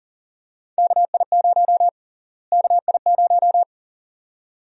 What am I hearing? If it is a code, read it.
Morse code, heard as a keyed tone, about 40 wpm, 700 Hz. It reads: KI0 KI0
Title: USA Call Sign Prefixes - Repeated 2x - 40wpm